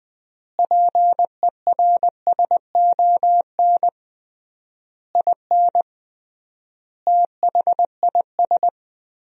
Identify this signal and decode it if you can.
Morse code, heard as a keyed tone, about 21 wpm, 700 Hz. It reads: PERSON IN THIS
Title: 3x Word-Spacing - Sentences from Top 400 Words - 20wpm